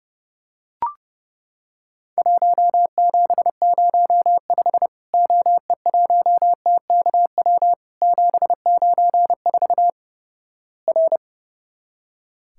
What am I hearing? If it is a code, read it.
Morse code, heard as a keyed tone, about 32 wpm, 700 Hz. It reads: 1705 OE1TKW 794 R
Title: Worked All Europe DX Contest - QTC Exchange - 30wpm